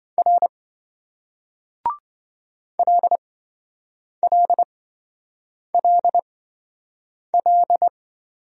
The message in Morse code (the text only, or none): R LLLL